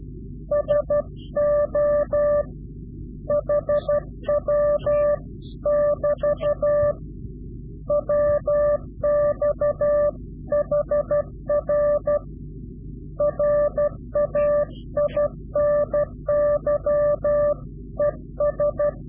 Morse code, heard as a keyed tone, about 12 wpm, 590 Hz. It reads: SO HW <BT> WX HR RAINY ES
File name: CW-QSO / Directional Call / Directional QSO